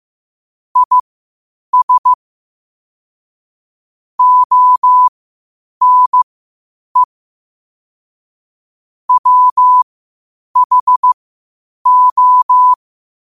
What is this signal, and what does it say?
Morse code, heard as a keyed tone, about 15 wpm, spaced as for 8 wpm, 1000 Hz. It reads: IS ONE WHO
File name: qotdfiles / 8wpm